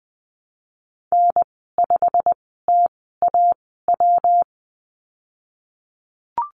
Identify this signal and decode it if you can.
Morse code, heard as a keyed tone, about 19 wpm, spaced as for 14 wpm, 700 Hz. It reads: N5TAW
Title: 167 Lesson 27g - Number 7 - Call Signs 20wpm